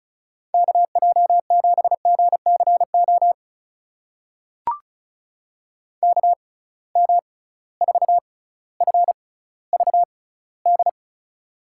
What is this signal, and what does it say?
Morse code, heard as a keyed tone, about 36 wpm, spaced as for 18 wpm, 700 Hz. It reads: KJ7GCO KM4FVD
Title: USA Call Signs - 6 Characters - 6x Character Spacing - 35wpm